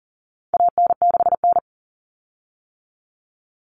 Morse code, heard as a keyed tone, about 40 wpm, 700 Hz. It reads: AN6N
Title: Common Cut Numbers - 4-Digit 40wpm